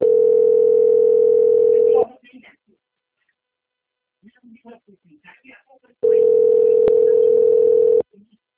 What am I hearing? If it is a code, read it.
Ringback tone (North America), 440 + 480 Hz, with speech.
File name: out- - - - - .59819